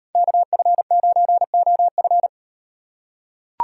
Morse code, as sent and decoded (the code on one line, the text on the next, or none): -.- ..-. ----. --- ..-.
KF9OF